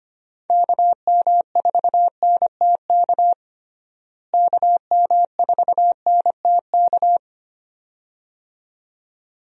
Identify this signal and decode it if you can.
Morse code, heard as a keyed tone, about 25 wpm, 700 Hz. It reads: KM4NTK KM4NTK
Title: USA Call Signs - 6 Characters - Repeated 2x - 25wpm